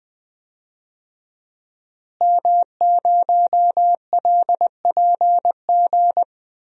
Morse code, heard as a keyed tone, about 20 wpm, 700 Hz. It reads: M0LPG
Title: DXCC-100 Call Signs - 5 Characters - Repeated 2x - 20wpm